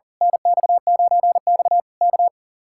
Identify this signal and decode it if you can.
Morse code, heard as a keyed tone, about 38 wpm, 700 Hz. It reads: NX9X K
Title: Speed-Racing Common QSO Phrases - 40wpm